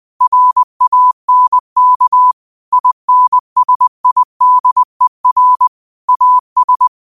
Morse code, heard as a keyed tone, about 20 wpm, 1000 Hz. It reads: RANK INSIDER AS
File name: qotdfiles / 20wpm